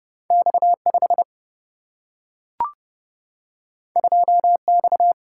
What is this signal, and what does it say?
Morse code, heard as a keyed tone, about 30 wpm, 700 Hz. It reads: X 5 2 X
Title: Single-Letter Single-Number 30wpm